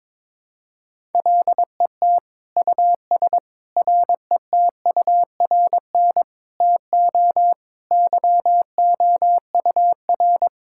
Morse code, heard as a keyed tone, about 22 wpm, 700 Hz. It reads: LET US RETURN TO YOUR
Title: Sentences from Top 1000 Words - Review Part 6 - 22wpm